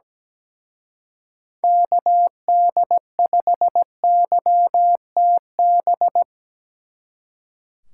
Morse code, heard as a keyed tone, about 20 wpm, 700 Hz. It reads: KD5YTB
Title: Speed Racing - USA 2x3 Call Signs 17wpm